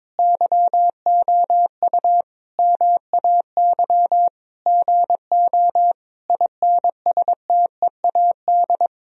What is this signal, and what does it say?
Morse code, heard as a keyed tone, about 22 wpm, 700 Hz. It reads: YOU MAY GO INSTEAD
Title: Sentences from Top 1000 Words - Review Part 4 - 22wpm